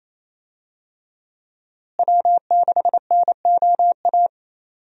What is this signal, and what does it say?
Morse code, heard as a keyed tone, about 28 wpm, 700 Hz. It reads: W6NOA
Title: USA Call Signs - 5 Characters - Repeated 2x - 28wpm